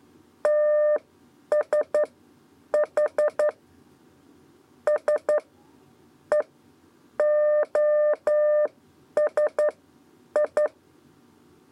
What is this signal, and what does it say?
Morse code, heard as a keyed tone, about 11 wpm, spaced as for 8 wpm, 600 Hz. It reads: TSH S E OSI
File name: Lezione trasmessa 2 lez 2 exe